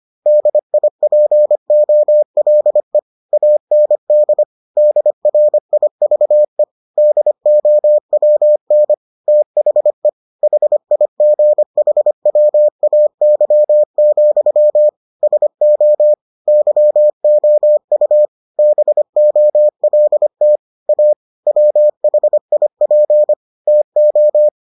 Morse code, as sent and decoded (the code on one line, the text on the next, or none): -.. .. .--. --- .-.. / .- -. -.. / -.. .-. .. ...- . / -.. --- .-- -. / - .... / .... .. --. .... .-- .- -.-- --..-- / ... --- / -.-- --- ..- / -... --- .-.. - / .- / .-- .... .. .--. / - ---
DIPOLE AND DRIVE DOWN THE HIGHWAY, SO YOU BOLT A WHIP TO